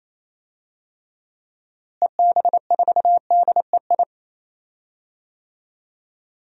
Morse code, heard as a keyed tone, about 28 wpm, 700 Hz. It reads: EB4DEI
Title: DXCC-100 Call Signs - 6 Characters - Repeated 3x - 28wpm